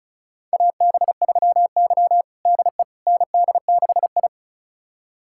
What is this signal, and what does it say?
Morse code, heard as a keyed tone, about 34 wpm, 700 Hz. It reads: AB3Y DE ND6I
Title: USA Callsign DE Callsign - 4 Characters - 35wpm